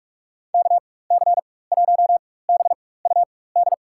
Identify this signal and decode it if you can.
Morse code, heard as a keyed tone, about 43 wpm, spaced as for 21 wpm, 700 Hz. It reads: KC1BUD
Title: USA Call Signs - 4x Character Spacing - 45wpm